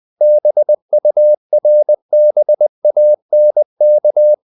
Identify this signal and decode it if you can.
Morse code, heard as a keyed tone, about 20 wpm, 600 Hz. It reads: BURBANK
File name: CW / Practice / EnglishQuotations0320